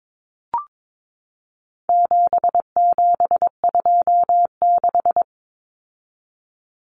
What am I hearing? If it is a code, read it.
Morse code, heard as a keyed tone, about 22 wpm, 700 Hz. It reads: 7726
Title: Numbers - 4-Digit 22wpm